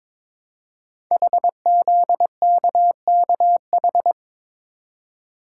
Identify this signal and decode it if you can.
Morse code, heard as a keyed tone, about 21 wpm, 700 Hz. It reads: HZKKH
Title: Code Groups - 5 Letters - 22wpm